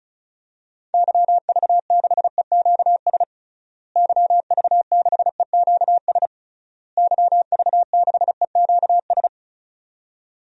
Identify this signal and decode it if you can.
Morse code, heard as a keyed tone, about 34 wpm, 700 Hz. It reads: YV6EQS YV6EQS YV6EQS
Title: DXCC-100 Call Signs - 6 Characters - Repeated 3x - 35wpm